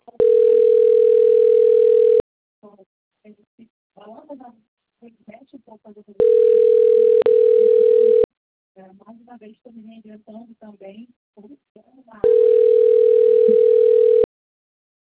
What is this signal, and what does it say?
Ringback tone (North America), 440 + 480 Hz, with speech.
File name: out- - - - - .32588